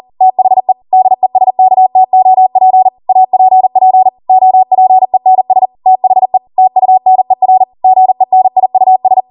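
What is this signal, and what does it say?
Morse code, heard as a keyed tone, about 40 wpm, 750 Hz. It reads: THE DESKTOP APP OPENS THE TUNER GENIUS
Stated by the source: ARRL 40 WPM Morse code practice transmission.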